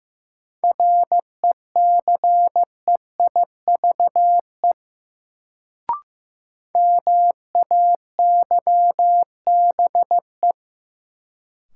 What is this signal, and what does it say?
Morse code, heard as a keyed tone, about 15 wpm, 700 Hz. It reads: RECEIVE MAYBE
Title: Sentences from Top 1000 Words - New Word Drill - 15wpm